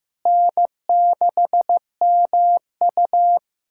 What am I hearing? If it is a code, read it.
Morse code, heard as a keyed tone, about 15 wpm, 700 Hz. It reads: N6MU
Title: K1USN SST QSO Practice - 15wpm